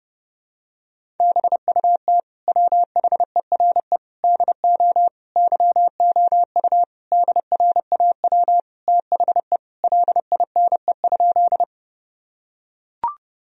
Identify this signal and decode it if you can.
Morse code, heard as a keyed tone, about 30 wpm, 700 Hz. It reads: BUT WHERE DO YOU DRAW THE LINE?
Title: Sentences from Top 1000 Words - Review Part 1 - 30wpm